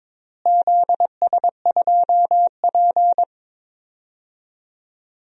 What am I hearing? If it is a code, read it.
Morse code, heard as a keyed tone, about 21 wpm, 700 Hz. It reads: ZS2P
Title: DXCC-100 Call Signs - 4 Characters - Repeated 2x - 22wpm